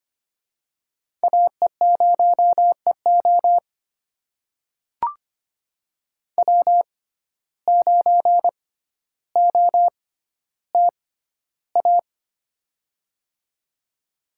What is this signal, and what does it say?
Morse code, heard as a keyed tone, about 25 wpm, spaced as for 13 wpm, 700 Hz. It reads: AE0EO W9OTA
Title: USA Call Signs - 5 Characters - 6x Character Spacing - 25wpm